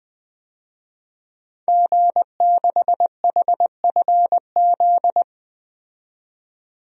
Morse code, heard as a keyed tone, about 20 wpm, 700 Hz. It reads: G6HFZ
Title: DXCC-100 Call Signs - 5 Characters - 20wpm